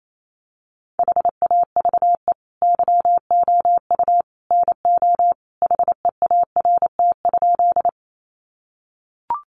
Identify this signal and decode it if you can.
Morse code, heard as a keyed tone, about 28 wpm, 700 Hz. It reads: HAVE YOU NO HEART?
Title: Sentences from Top 1200 Words 28wpm